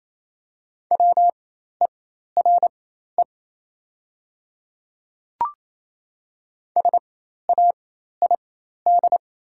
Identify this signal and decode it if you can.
Morse code, heard as a keyed tone, about 28 wpm, spaced as for 13 wpm, 700 Hz. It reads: WERE SAID
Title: 4x Character Spacing - Top 100 Words - 28wpm